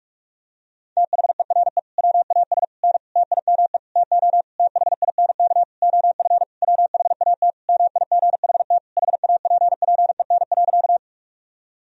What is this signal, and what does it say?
Morse code, heard as a keyed tone, about 46 wpm, 700 Hz. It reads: THERE WAS N TIME TO THINK OF WHAT MIGHT HAPPEN.
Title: Sentences from Top 400 Words and Abbreviations 45wpm